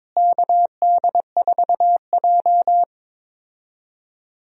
Morse code, heard as a keyed tone, about 21 wpm, 700 Hz. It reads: KD4J
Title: USA Call Signs - 4 Characters - Repeated 2x - 22wpm